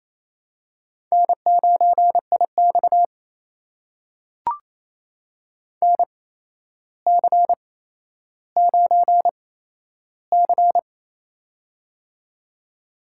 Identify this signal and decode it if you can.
Morse code, heard as a keyed tone, about 29 wpm, spaced as for 12 wpm, 700 Hz. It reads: N9IX NC9C
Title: USA Call Signs - 4 Characters - 8x Character Spacing - 28wpm